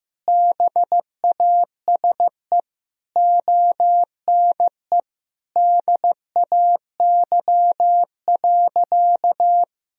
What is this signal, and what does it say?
Morse code, heard as a keyed tone, about 15 wpm, 700 Hz. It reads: BASE ONE DAY.
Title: Sentences from Top 1000 Words - Review Part 1 - 15wpm